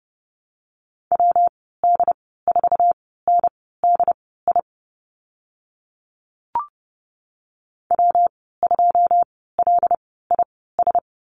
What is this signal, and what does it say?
Morse code, heard as a keyed tone, about 30 wpm, spaced as for 17 wpm, 700 Hz. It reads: WD4NDI W2LIS
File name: Lesson 19g - Letter w - Call Signs 30wpm